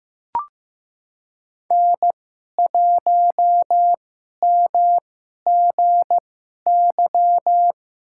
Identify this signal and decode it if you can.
Morse code, heard as a keyed tone, about 15 wpm, spaced as for 11 wpm, 700 Hz. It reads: N1MGY
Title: USA Call Signs - 5 Characters - 2x Character Spacing - 15wpm